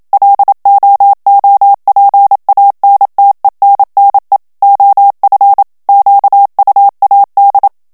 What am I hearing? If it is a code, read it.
Morse code, heard as a keyed tone, about 27 wpm, 800 Hz. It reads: LOOPANTENNE OF QUAD
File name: HOLLANDAIS texte clair 1 vitesse 26 mots minute